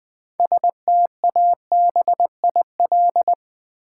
Morse code, heard as a keyed tone, about 20 wpm, 700 Hz. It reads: STABIL
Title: Build A Word - 3 Letters At A Time - 9-Letter Words - 20wpm